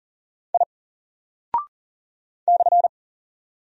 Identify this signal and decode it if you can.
Morse code, heard as a keyed tone, about 40 wpm, 700 Hz. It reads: I /
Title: Single Character (Letter:Number:Punctuation:Prosign) 40wpm